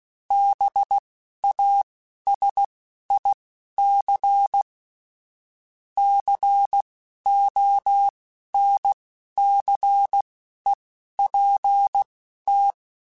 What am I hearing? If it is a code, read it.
Morse code, heard as a keyed tone, about 16 wpm, spaced as for 11 wpm, 800 Hz. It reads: BASIC CONCEPT
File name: Technical Subjects / Technical Subjects-mp3s / AND Gate @10